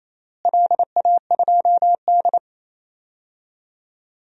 Morse code, heard as a keyed tone, about 28 wpm, 700 Hz. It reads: LA2D
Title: DXCC-100 Call Signs - 4 Characters - Repeated 3x - 28wpm